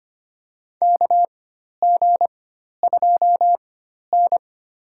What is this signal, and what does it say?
Morse code, heard as a keyed tone, about 25 wpm, spaced as for 12 wpm, 700 Hz. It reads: KG2N
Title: USA Call Signs - 4 Characters - 4x Character Spacing - 25wpm